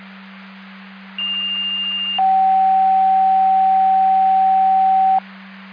Two-tone page, heard as a one-second tone, then a three-second tone, about 2800 then 760 Hz. It is heard from 1.2 to 5.2 s.